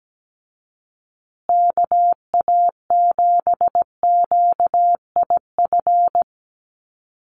Morse code, heard as a keyed tone, about 17 wpm, 700 Hz. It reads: KA7QIF
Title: USA Call Signs Repeated 2x 17wpm